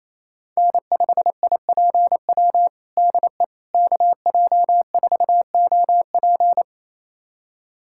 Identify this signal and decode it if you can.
Morse code, heard as a keyed tone, about 28 wpm, 700 Hz. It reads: N5IPW DE KJ4OP
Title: USA Callsign DE Callsign - 5 Characters - 28wpm